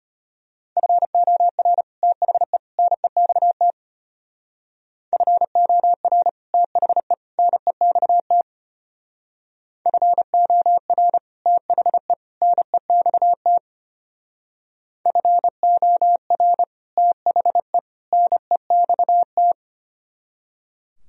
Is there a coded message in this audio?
Yes: Morse code from 0.8 to 19.5 s, heard as a keyed tone, about 31 wpm, 700 Hz. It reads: FOR THE NEXT FOR THE NEXT FOR THE NEXT FOR THE NEXT